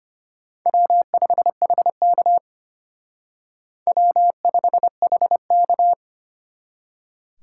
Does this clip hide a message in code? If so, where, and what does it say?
Yes: Morse code from 0.7 to 5.9 s, heard as a keyed tone, about 32 wpm, 700 Hz. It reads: W5HK W5HK